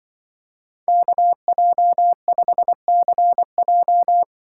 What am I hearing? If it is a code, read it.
Morse code, heard as a keyed tone, about 24 wpm, 700 Hz. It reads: KJ5CJ
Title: Speed Racing - USA Call Signs 20wpm